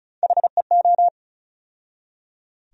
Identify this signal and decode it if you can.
Morse code, heard as a keyed tone, about 36 wpm, 700 Hz. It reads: HEO
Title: USA Call Sign Postfixes - 3 Letters 35wpm